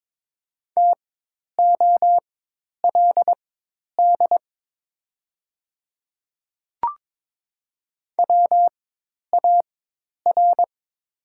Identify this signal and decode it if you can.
Morse code, heard as a keyed tone, about 21 wpm, spaced as for 10 wpm, 690 Hz. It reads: TOLD WAR
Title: 4x Character Spacing - Top 500 Words - 22wpm